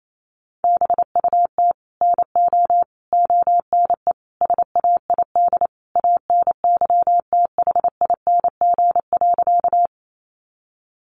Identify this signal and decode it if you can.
Morse code, heard as a keyed tone, about 28 wpm, 700 Hz. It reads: BUT NO ONE SAID ANYTHING.
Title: Sentences from Top 1000 Words 28wpm